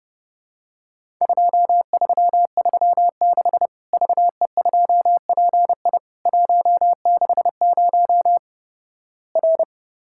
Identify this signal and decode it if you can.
Morse code, heard as a keyed tone, about 30 wpm, 700 Hz. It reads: 2336 VE2PI 160 R
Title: Worked All Europe DX Contest - QTC Exchange - 30wpm